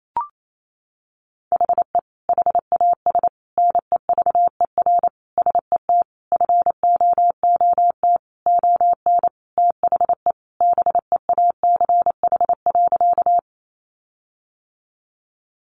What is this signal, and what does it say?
Morse code, heard as a keyed tone, about 28 wpm, 700 Hz. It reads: HE HAS NEVER SET FOOT ON THE BEACH.
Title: Sentences from Top 1500 Words 28wpm